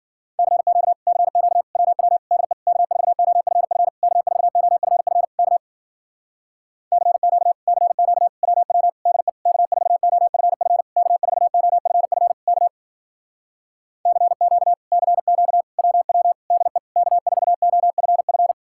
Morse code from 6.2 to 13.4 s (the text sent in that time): CQ CQ WW DE K4OFF K4OFF K